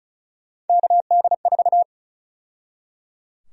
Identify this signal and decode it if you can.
Morse code, heard as a keyed tone, about 34 wpm, 700 Hz. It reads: KD4
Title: USA Call Sign Prefixes - 35wpm